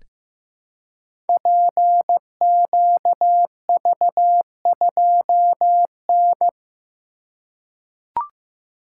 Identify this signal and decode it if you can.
Morse code, heard as a keyed tone, about 16 wpm, 700 Hz. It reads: PQV2N
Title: Code Groups - 5 Letter:Numbers - 15wpm